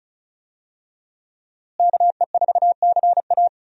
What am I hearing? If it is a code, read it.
Morse code, heard as a keyed tone, about 36 wpm, 700 Hz. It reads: KE4CA
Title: USA Call Signs - 5 Characters - Repeated 2x - 35wpm